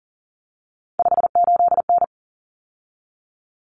Morse code, heard as a keyed tone, about 40 wpm, 700 Hz. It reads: 58N
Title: Common Cut Numbers - 3-Digit 40wpm